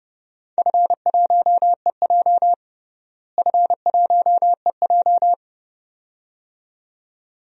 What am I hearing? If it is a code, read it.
Morse code, heard as a keyed tone, about 30 wpm, 700 Hz. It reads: F1EJ F1EJ
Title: DXCC-100 Call Signs - 4 Characters - Repeated 2x - 30wpm